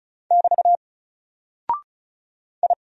Morse code, heard as a keyed tone, about 36 wpm, 700 Hz. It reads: <BT> I
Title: Single Character (Letter:Number:Punctuation:Prosign) 35wpm